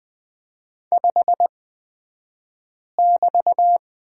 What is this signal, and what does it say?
Morse code, heard as a keyed tone, about 20 wpm, spaced as for 5 wpm, 700 Hz. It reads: 5<BT>
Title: 223 Lesson 36d - BT - Two Characters 20wpm